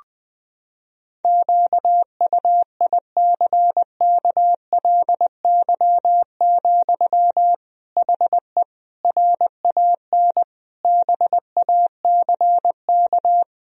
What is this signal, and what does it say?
Morse code, heard as a keyed tone, about 21 wpm, 700 Hz. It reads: QUICKLY, HE RAN BACK
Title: Sentences from Top 1000 Words - Review Part 3 - 20wpm